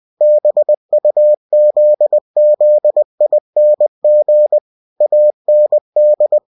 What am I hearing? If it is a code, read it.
Morse code, heard as a keyed tone, about 20 wpm, 600 Hz. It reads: BUZZING AND
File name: cw-books / winnie-the-pooh / winnie-20wpm-page-0002